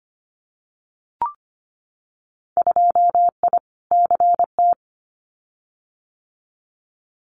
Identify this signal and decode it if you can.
Morse code, heard as a keyed tone, about 25 wpm, 690 Hz. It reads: 2I CT
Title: Winter Field Day Exchanges 25wpm